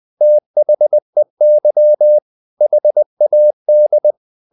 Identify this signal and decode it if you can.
Morse code, heard as a keyed tone, about 20 wpm, 600 Hz. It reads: THEY HAD
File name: the old man and the sea / the old man and the sea part 1 20wpm